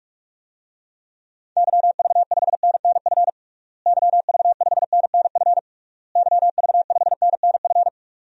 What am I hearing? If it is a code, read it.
Morse code, heard as a keyed tone, about 46 wpm, 700 Hz. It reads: YV5NNF YV5NNF YV5NNF
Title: DXCC-100 Call Signs - 6 Characters - Repeated 3x - 45wpm